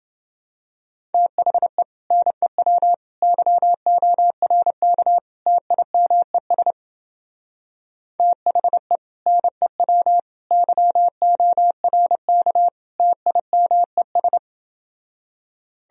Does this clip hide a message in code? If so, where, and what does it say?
Yes: Morse code from 1.1 to 14.4 s, heard as a keyed tone, about 29 wpm, 700 Hz. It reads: THE NEW YORK TIMES THE NEW YORK TIMES